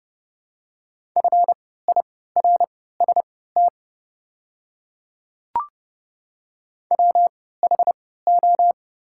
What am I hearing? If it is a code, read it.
Morse code, heard as a keyed tone, about 30 wpm, spaced as for 17 wpm, 700 Hz. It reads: FIRST WHO
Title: 3x Character Spacing - Top 100 Words - 30wpm